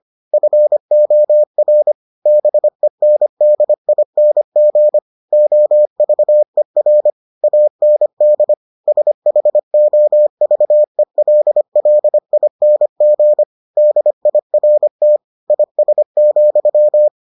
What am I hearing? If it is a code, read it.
Morse code, heard as a keyed tone, about 25 wpm, 600 Hz. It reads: FOR BENDING OVER AND SHOVELLING DIRT IS,